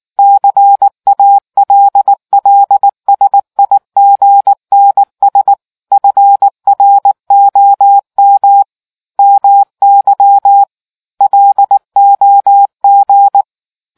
Morse code, heard as a keyed tone, about 19 wpm, 800 Hz. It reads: CALLSIGNS FROM MY LOG